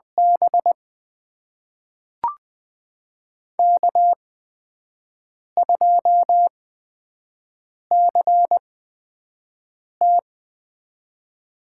Morse code, heard as a keyed tone, about 20 wpm, spaced as for 5 wpm, 700 Hz. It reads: B K2CT
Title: USA Call Signs - 4 Characters - 8x Character Spacing - 20wpm